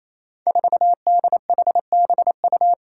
Morse code, heard as a keyed tone, about 29 wpm, 700 Hz. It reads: 4DHBU
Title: Five Challenging Letter-Number Combinations 28wpm